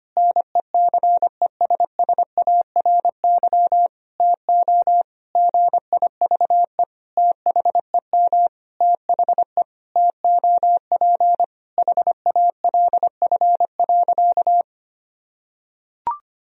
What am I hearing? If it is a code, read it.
Morse code, heard as a keyed tone, about 25 wpm, 700 Hz. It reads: NECESSARY TO GIVE THEM THE TOP HALF.
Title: Sentences from Top 1000 Words - Review Part 2 - 25wpm